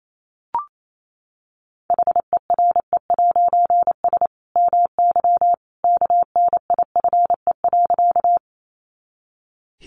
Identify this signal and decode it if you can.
Morse code, heard as a keyed tone, about 28 wpm, 700 Hz. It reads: HERE'S MY KNIFE.
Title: A Sound Of Thunder 28wpm